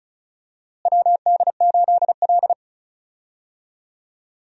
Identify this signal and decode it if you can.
Morse code, heard as a keyed tone, about 36 wpm, 700 Hz. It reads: WD8L
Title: USA Call Signs - 4 Characters - Repeated 2x - 35wpm